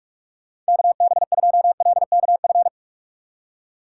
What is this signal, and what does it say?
Morse code, heard as a keyed tone, about 59 wpm, 700 Hz. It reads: KB2LKF